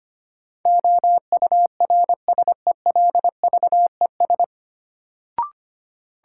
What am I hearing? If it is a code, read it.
Morse code, heard as a keyed tone, about 25 wpm, 700 Hz. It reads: OURSELVES
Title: Sentences from Top 1600 Words - New Word Drill - 25wpm